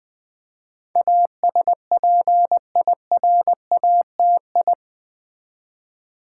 Morse code, heard as a keyed tone, about 20 wpm, 700 Hz. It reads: ASPIRATI
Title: Build A Word - 4 Letters At A Time - 10-Letter Words - 20wpm